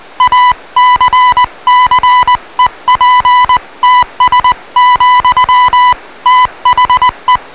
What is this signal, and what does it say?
Morse code, heard as a keyed tone, about 20 wpm, 1000 Hz. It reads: ACCEPTS, THE